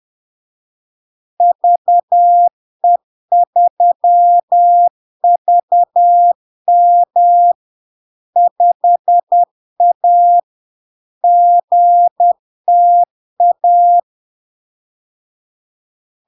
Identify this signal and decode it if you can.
Morse code, heard as a keyed tone, about 10 wpm, 700 Hz. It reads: VE3VM 5A GTA
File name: LARK-FD-2022-Log-in-CW-10wpm-700-Hz